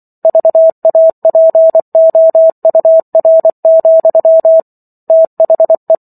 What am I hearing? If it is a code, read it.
Morse code, heard as a keyed tone, about 24 wpm, 650 Hz. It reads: VAPOUR, THE
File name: Chapter 15 - 24 WPM